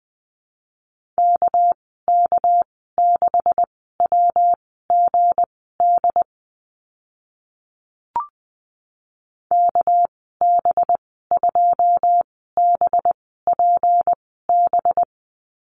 Morse code, heard as a keyed tone, about 20 wpm, spaced as for 15 wpm, 700 Hz. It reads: KK6WGD KB2BPB
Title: 200 Lesson 32g - Letter k - Call Signs 20wpm